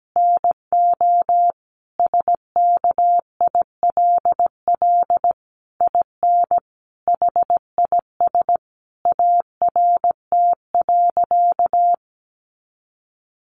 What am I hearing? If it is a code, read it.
Morse code, heard as a keyed tone, about 17 wpm, 700 Hz. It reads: NO SKILL IN HIS ART.
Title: Sentences from Top 1200 Words 17wpm